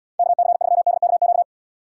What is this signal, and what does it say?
Morse code, heard as a keyed tone, about 76 wpm, 700 Hz. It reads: KC3DKQ